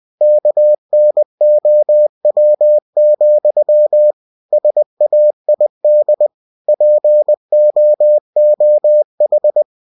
Morse code, heard as a keyed tone, about 20 wpm, 600 Hz. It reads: KNOW, SAID POOH
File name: cw-books / winnie-the-pooh / winnie-20wpm-page-0061